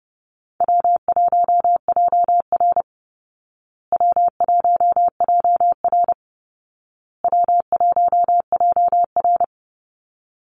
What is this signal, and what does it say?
Morse code, heard as a keyed tone, about 30 wpm, 700 Hz. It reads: W1JR W1JR W1JR
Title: Easy Call Signs Repeated 3x - 30wpm